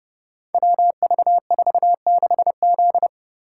Morse code, heard as a keyed tone, about 29 wpm, 700 Hz. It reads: WV46Z
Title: Code Groups - 5 Letter:Numbers - 30wpm